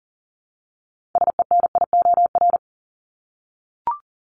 Morse code, heard as a keyed tone, about 40 wpm, 700 Hz. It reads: SENIOR
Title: Top 1751-2000 Words 40wpm